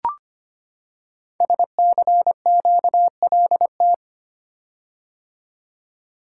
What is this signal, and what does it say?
Morse code, heard as a keyed tone, about 25 wpm, 700 Hz. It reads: SCQLT
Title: Code Groups - 5 Letters - 25wpm